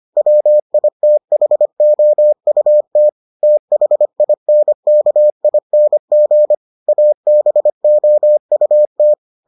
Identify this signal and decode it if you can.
Morse code, heard as a keyed tone, about 25 wpm, 600 Hz. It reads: WITHOUT THINKING ABOUT